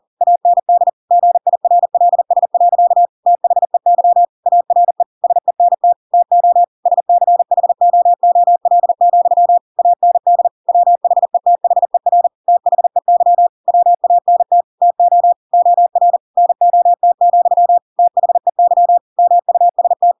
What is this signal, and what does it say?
Morse code, heard as a keyed tone, about 40 wpm, 700 Hz. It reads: AND GIRLS. THEY ARE SENT TO SCHOOL, AND WHETHER THEY WANT TO OR NOT, THEY MUST
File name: Pinocchio / Pinocchio ch4 40 40 ik1hge